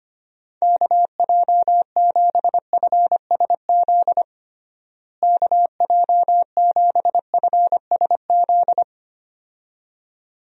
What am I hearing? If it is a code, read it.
Morse code, heard as a keyed tone, about 25 wpm, 700 Hz. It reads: KJ7FSZ KJ7FSZ
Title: USA Call Signs - 6 Characters - Repeated 2x - 25wpm